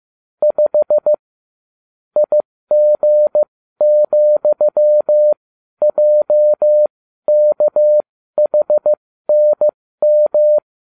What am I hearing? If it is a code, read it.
Morse code, heard as a keyed tone, about 15 wpm, spaced as for 12 wpm, 600 Hz. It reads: 5 IG,JKHNM